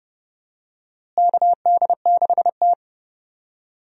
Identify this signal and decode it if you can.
Morse code, heard as a keyed tone, about 30 wpm, 700 Hz. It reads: KD6T
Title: USA Call Signs - 4 Characters - Repeated 2x - 30wpm